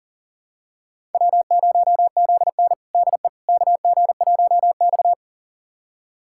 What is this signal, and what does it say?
Morse code, heard as a keyed tone, about 40 wpm, 700 Hz. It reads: W0ZN DE KG1X
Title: USA Callsign DE Callsign - 4 Characters - 40wpm